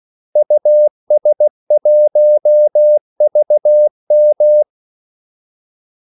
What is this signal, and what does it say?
Morse code, heard as a keyed tone, about 16 wpm, 600 Hz. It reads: US1VM
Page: Kaldesignaler 16 wpm